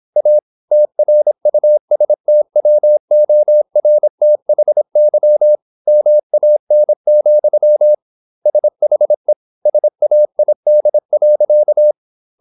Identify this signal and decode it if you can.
Morse code, heard as a keyed tone, about 26 wpm, 600 Hz. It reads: A TRUSTWORTHY MAN, SHE SAID.
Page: Never 26wpm | CW med Gnister